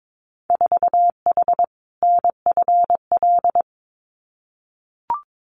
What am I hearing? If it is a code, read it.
Morse code, heard as a keyed tone, about 22 wpm, 700 Hz. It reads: 4H NFL
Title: Winter Field Day Exchanges Encore 22wpm